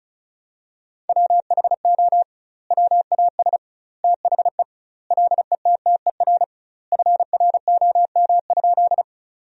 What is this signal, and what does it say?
Morse code, heard as a keyed tone, about 36 wpm, 700 Hz. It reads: WHO WAS THE LETTER FROM?
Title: 2x Word-Spacing - Sentences from Top 300 Words - 35wpm